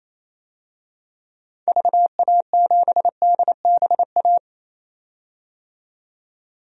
Morse code, heard as a keyed tone, about 28 wpm, 700 Hz. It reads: VA7DBA
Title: DXCC-100 Call Signs - 6 Characters - 28wpm